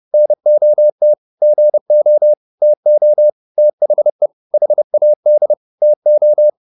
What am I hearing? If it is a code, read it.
Morse code, heard as a keyed tone, about 30 wpm, 600 Hz. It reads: NOT GO TO THE HAD TO